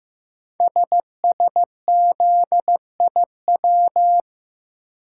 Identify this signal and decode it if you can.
Morse code, heard as a keyed tone, about 15 wpm, 690 Hz. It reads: SSZIW
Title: Code Groups - 5 Letters - 15wpm